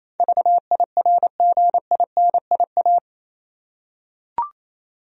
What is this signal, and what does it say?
Morse code, heard as a keyed tone, about 28 wpm, 700 Hz. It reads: VIRGINIA
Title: USA Common Names - 28wpm